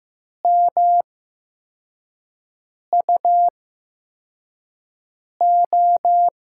Morse code, heard as a keyed tone, about 15 wpm, spaced as for 4 wpm, 700 Hz. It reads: MUO
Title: USA Call Signs - 6 Characters - 8x Character Spacing - 15wpm